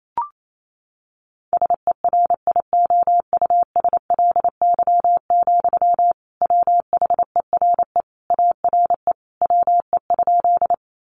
Morse code, heard as a keyed tone, about 28 wpm, 700 Hz. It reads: SERIOUSLY, WHERE ARE WE?
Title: Sentences from Top 1200 Words 28wpm